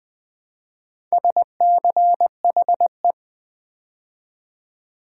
Morse code, heard as a keyed tone, about 20 wpm, 700 Hz. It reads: SCHE
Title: Build A Word - 2 Letters At A Time - 8-Letter Words - 20wpm